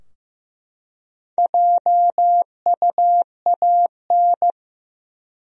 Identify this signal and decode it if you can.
Morse code, heard as a keyed tone, about 15 wpm, 700 Hz. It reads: JUAN
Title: USA Common Names - Max 4 Letters - 15wpm